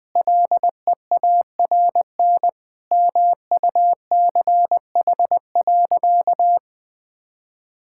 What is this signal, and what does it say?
Morse code, heard as a keyed tone, about 20 wpm, 700 Hz. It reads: LEARN MUCH.
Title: Sentences with 2-Word Phrases - 20wpm